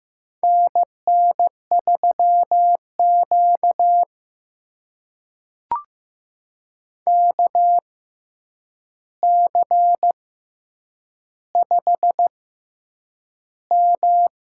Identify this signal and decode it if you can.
Morse code, heard as a keyed tone, about 15 wpm, spaced as for 8 wpm, 700 Hz. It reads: NN3Q KC5M
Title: USA Call Signs - 4 Characters - 6x Character Spacing - 15wpm